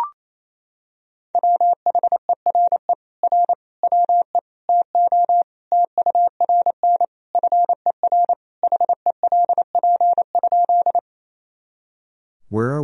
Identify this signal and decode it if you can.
Morse code, heard as a keyed tone, about 28 wpm, 700 Hz. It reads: WHERE R WE TO TURN FER HELP?
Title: Sentences from Top 200 Words and Abbreviations 28wpm